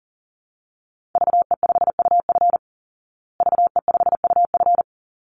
Morse code, heard as a keyed tone, about 40 wpm, 700 Hz. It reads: VE5UF VE5UF
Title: Call Signs Repeated 3x - 40wpm